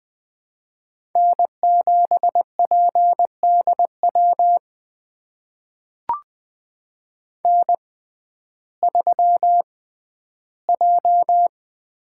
Morse code, heard as a keyed tone, about 20 wpm, spaced as for 12 wpm, 700 Hz. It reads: N7PDW N3J
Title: USA Call Signs - 5 Characters - 6x Character Spacing - 20wpm